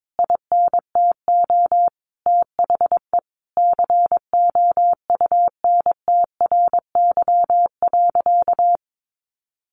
Morse code, heard as a keyed tone, about 22 wpm, 700 Hz. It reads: INTO THE COUNTRY.
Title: Sentences from Top 1200 Words 22wpm